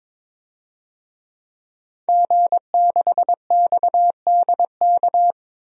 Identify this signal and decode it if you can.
Morse code, heard as a keyed tone, about 22 wpm, 700 Hz. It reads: G6XDK
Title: DXCC-100 Call Signs - 5 Characters - 22wpm